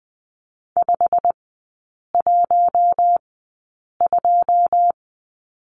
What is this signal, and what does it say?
Morse code, heard as a keyed tone, about 20 wpm, spaced as for 8 wpm, 700 Hz. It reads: 512
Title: 133 Lesson 22e - Question Mark - Three Characters 20wpm